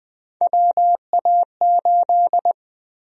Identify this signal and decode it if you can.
Morse code, heard as a keyed tone, about 20 wpm, 700 Hz. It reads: WA8
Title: USA Call Sign Prefixes - Repeated 2x - 20wpm